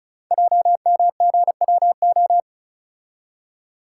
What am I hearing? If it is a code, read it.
Morse code, heard as a keyed tone, about 36 wpm, 700 Hz. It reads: JMGWO
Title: Code Groups - 5 Letters - 35wpm